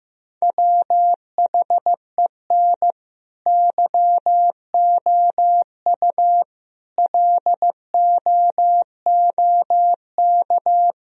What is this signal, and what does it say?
Morse code, heard as a keyed tone, about 16 wpm, 700 Hz. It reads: WHEN YOU LOOK
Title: Speed-Racing - Sets of 5 Words 15wpm